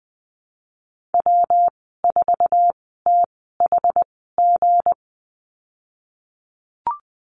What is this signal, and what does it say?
Morse code, heard as a keyed tone, about 20 wpm, spaced as for 15 wpm, 700 Hz. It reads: W4THG
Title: 179 Lesson 29g - Slash - Call Signs 20wpm